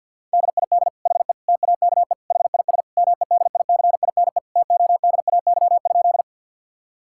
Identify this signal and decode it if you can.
Morse code, heard as a keyed tone, about 51 wpm, 700 Hz. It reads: DID HE TAKE HIS MEDICINE TODAY?
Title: Sentences from Top 1000 Words - Review Part 6 - 50wpm